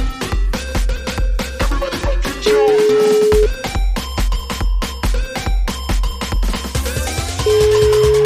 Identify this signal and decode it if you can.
Ringback tone (Europe, 425 Hz), with noise.